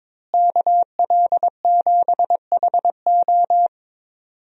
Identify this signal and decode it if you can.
Morse code, heard as a keyed tone, about 22 wpm, 700 Hz. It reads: KL7HO
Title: DXCC-100 Call Signs - 5 Characters - Repeated 2x - 22wpm